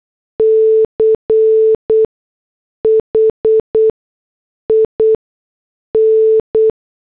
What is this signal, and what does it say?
Morse code, heard as a keyed tone, about 8 wpm, 440 Hz. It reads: C H I N
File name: morse / headlines